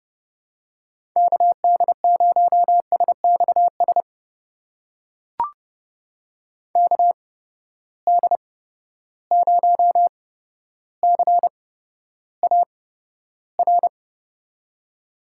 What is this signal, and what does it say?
Morse code, heard as a keyed tone, about 30 wpm, spaced as for 13 wpm, 700 Hz. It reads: KD0SXS KD0CAR
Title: USA Call Signs - 6 Characters - 8x Character Spacing - 30wpm